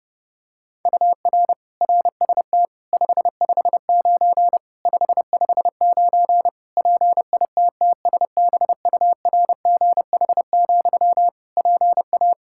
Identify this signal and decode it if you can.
Morse code, heard as a keyed tone, about 32 wpm, 700 Hz. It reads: UR RST 559 559 PITTSBURGH, PA
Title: Practice QSO - Exchange 1 - Beginning - 30wpm